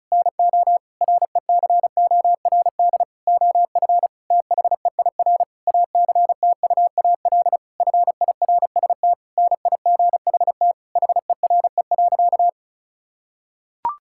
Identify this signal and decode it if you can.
Morse code, heard as a keyed tone, about 34 wpm, 700 Hz. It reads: NO RECORD OF THEIR ACTUAL FIRST NIGHT HERE.
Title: Sentences from Top 1000 Words - Review Part 5 - 35wpm